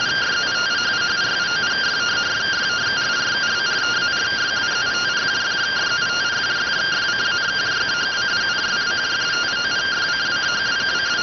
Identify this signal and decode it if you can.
Two-tone teleprinter signal, about 170 Hz shift.